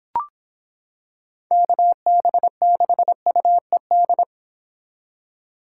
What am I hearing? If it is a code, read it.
Morse code, heard as a keyed tone, about 26 wpm, 700 Hz. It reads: KB6UED